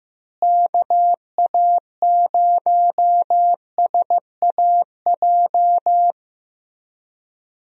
Morse code, heard as a keyed tone, about 15 wpm, 700 Hz. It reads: KA0SAJ
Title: USA Call Signs - 6 Characters - Repeated 3x - 15wpm